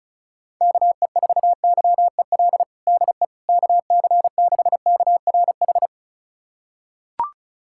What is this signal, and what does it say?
Morse code, heard as a keyed tone, about 34 wpm, 700 Hz. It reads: KE4YEL DE KC6KRH
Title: USA Callsign DE Callsign 35wpm